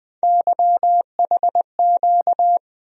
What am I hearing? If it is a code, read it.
Morse code, heard as a keyed tone, about 20 wpm, 700 Hz. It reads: YHQ
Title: USA Call Sign Postfixes - 3 Letters 20wpm